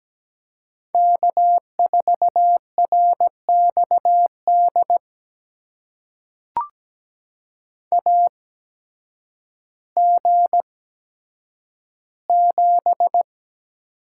Morse code, heard as a keyed tone, about 17 wpm, spaced as for 9 wpm, 700 Hz. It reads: K4RXD AG7
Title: USA Call Signs - 5 Characters - 8x Character Spacing - 17wpm